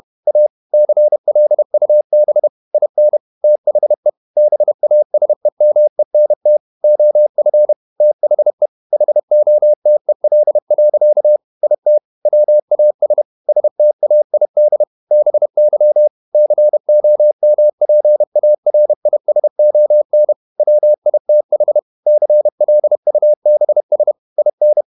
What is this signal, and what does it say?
Morse code, heard as a keyed tone, about 31 wpm, 600 Hz. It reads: A CLUB IN THE BASEMENT OF THE HOTEL. IT WAS STAID BY COMPARISON WITH CLUBS IN
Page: Never 31wpm | CW med Gnister